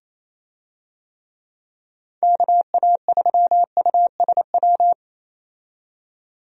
Morse code, heard as a keyed tone, about 28 wpm, 700 Hz. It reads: KA3USW
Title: USA Call Signs - 6 Characters - 28wpm